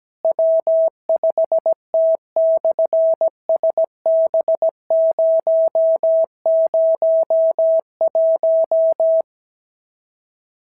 Morse code, heard as a keyed tone, about 17 wpm, 650 Hz. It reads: W5T/SB001
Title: SOTA QSO Practice 17wpm